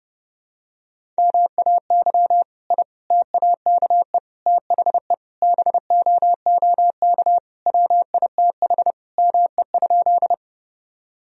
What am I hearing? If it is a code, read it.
Morse code, heard as a keyed tone, about 32 wpm, 700 Hz. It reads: MAY I TAKE THE BOOK WITH ME?
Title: Sentences from Top 1000 Words - Review Part 1 - 30wpm